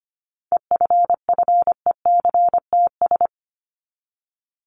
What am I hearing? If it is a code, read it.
Morse code, heard as a keyed tone, about 25 wpm, 700 Hz. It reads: EFFECTS
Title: ES:S Suffix - Top 100 Words with Base Words 25wpm